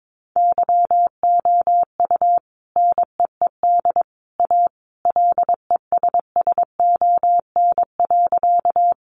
Morse code, heard as a keyed tone, about 22 wpm, 700 Hz. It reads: YOU NEED A LESSON.
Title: Sentences from Top 1200 Words 22wpm